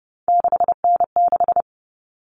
Morse code, heard as a keyed tone, about 28 wpm, 700 Hz. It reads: 6N6
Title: All Cut Numbers - 3-Digit 30wpm